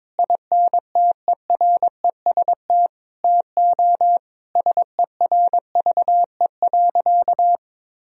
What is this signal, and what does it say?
Morse code, heard as a keyed tone, about 22 wpm, 700 Hz. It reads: INTEREST TO SERVE.
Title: Sentences from Top 1000 Words - Review Part 4 - 22wpm